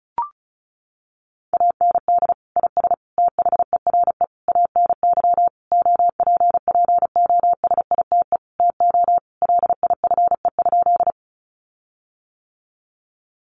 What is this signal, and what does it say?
Morse code, heard as a keyed tone, about 34 wpm, 700 Hz. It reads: AND IS THERE ANY OPPOSITE TO LIFE?
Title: Sentences from Top 1300 Words 35wpm